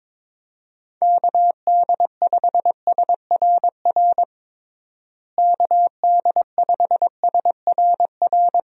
Morse code, heard as a keyed tone, about 22 wpm, 700 Hz. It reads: KD5SRR KD5SRR
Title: USA Call Signs - 6 Characters - Repeated 3x - 22wpm